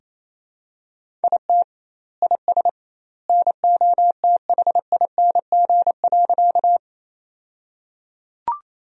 Morse code, heard as a keyed tone, about 27 wpm, 700 Hz. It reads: IT IS NOTHING.
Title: 2x Word-Spacing - Sentences from Top 500 Words - 28wpm